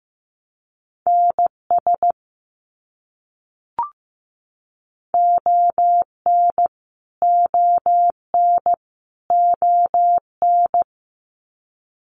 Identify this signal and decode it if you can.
Morse code, heard as a keyed tone, about 15 wpm, 700 Hz. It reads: NS ON ON ON
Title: Canadian Provinces and Territories - Repeated 3x - 15wpm